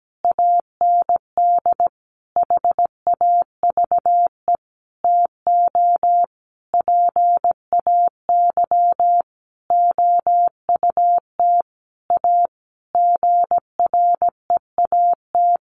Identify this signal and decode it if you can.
Morse code, heard as a keyed tone, about 17 wpm, 700 Hz. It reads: AND HAVE TO PAY OUT A GREAT
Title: Sentences from Top 1500 Words 17wpm